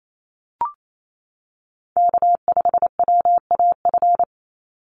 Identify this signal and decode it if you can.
Morse code, heard as a keyed tone, about 28 wpm, 700 Hz. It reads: K5WAF
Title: Call Signs Repeated 2x - 28wpm